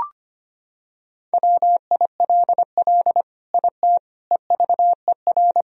Morse code, heard as a keyed tone, about 25 wpm, 700 Hz. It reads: WILL IT EVER
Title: Sentences from Top 1000 Words - Review Part 2 - 25wpm